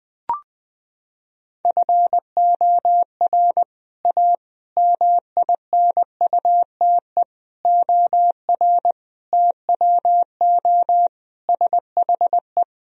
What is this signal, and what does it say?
Morse code, heard as a keyed tone, about 21 wpm, 700 Hz. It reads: FOR A MINUTE OR TWO SHE
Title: Sentences from Top 1000 Words - Review Part 1 - 20wpm